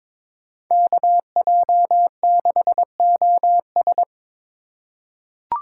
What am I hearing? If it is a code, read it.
Morse code, heard as a keyed tone, about 22 wpm, 700 Hz. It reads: KJ6OS
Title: USA Call Signs - 5 Characters - 2x Character Spacing - 22wpm